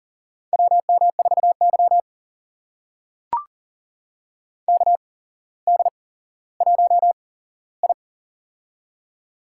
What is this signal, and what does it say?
Morse code, heard as a keyed tone, about 40 wpm, spaced as for 17 wpm, 700 Hz. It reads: WM4Y KD1I
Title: USA Call Signs - 4 Characters - 8x Character Spacing - 40wpm